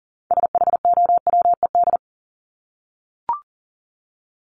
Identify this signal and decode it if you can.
Morse code, heard as a keyed tone, about 40 wpm, 700 Hz. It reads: SHOWED
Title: ED Suffix - Top 100 Words with Base Words 40wpm